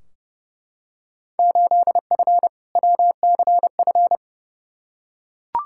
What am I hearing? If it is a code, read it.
Morse code, heard as a keyed tone, about 31 wpm, 700 Hz. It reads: 8F WCF
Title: Field Day Exchanges Encore - 30wpm